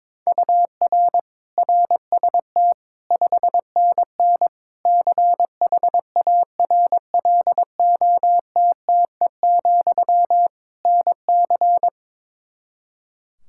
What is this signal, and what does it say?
Morse code, heard as a keyed tone, about 22 wpm, 700 Hz. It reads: UR RST 5NN CHARLOTTE, NC
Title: K1USN SST QSO Practice 22wpm